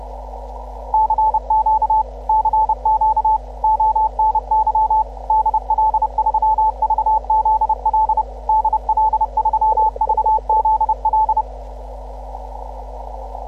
Morse code, heard as a keyed tone, about 30 wpm, 870 Hz. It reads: CQ CQ ONY DL3VZL DL3VZL